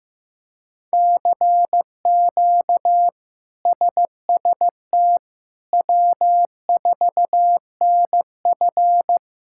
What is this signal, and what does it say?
Morse code, heard as a keyed tone, about 15 wpm, 700 Hz. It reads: CQ SST W4NF
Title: K1USN SST QSO Practice - 15wpm